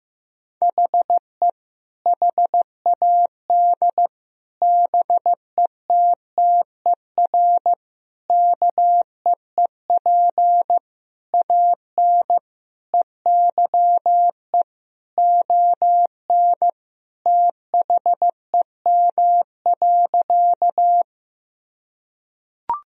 Morse code, heard as a keyed tone, about 15 wpm, 700 Hz. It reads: HE HAD BETTER KEEP AN EYE ON THEM.
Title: Sentences from Top 1000 Words - Review Part 1 - 15wpm